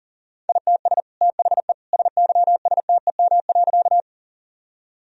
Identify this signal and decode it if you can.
Morse code, heard as a keyed tone, about 40 wpm, 700 Hz. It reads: ITS THE SYSTEM.
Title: Sentences from Top 1000 Words - Review Part 3 - 40wpm